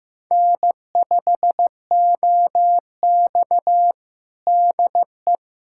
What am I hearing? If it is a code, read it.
Morse code, heard as a keyed tone, about 15 wpm, 700 Hz. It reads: N5OX DE
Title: USA Callsign DE Callsign - 4 Characters - 15wpm